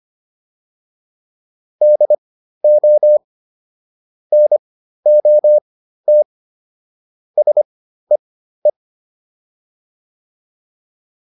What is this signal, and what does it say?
Morse code, heard as a keyed tone, about 25 wpm, spaced as for 13 wpm, 600 Hz. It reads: DO NOT SEE